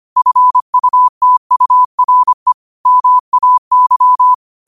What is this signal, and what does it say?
Morse code, heard as a keyed tone, about 25 wpm, 1000 Hz. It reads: FUTURE MAY
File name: qotdfiles / 25wpm